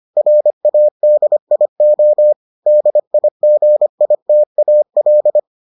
Morse code, heard as a keyed tone, about 25 wpm, 600 Hz. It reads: RADIO DIGITAL